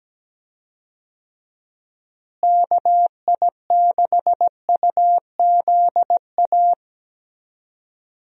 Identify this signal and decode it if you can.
Morse code, heard as a keyed tone, about 17 wpm, 700 Hz. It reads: KI6UZA
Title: USA Call Signs - 6 Characters - Repeated 2x - 17wpm